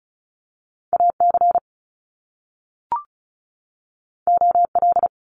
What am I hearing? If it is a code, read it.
Morse code, heard as a keyed tone, about 36 wpm, 700 Hz. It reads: AC OL
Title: Top Two Letter Combinations in Words 35wpm